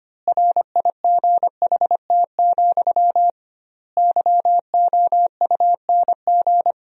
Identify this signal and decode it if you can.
Morse code, heard as a keyed tone, about 25 wpm, 700 Hz. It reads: RIGHT, YOUNG
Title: 2x Word-Spacing - Sentences from Top 400 Words - 25wpm